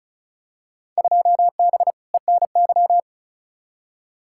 Morse code, heard as a keyed tone, about 36 wpm, 700 Hz. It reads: 2B ENY
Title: Field Day Exchanges Encore - 35wpm